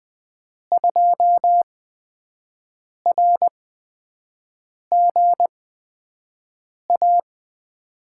Morse code, heard as a keyed tone, about 20 wpm, spaced as for 5 wpm, 700 Hz. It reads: 2RGA
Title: USA Call Signs - 5 Characters - 8x Character Spacing - 20wpm